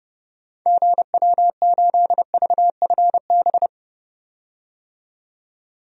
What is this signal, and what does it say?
Morse code, heard as a keyed tone, about 30 wpm, 700 Hz. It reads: GW8VFB
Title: DXCC-100 Call Signs - 6 Characters - 30wpm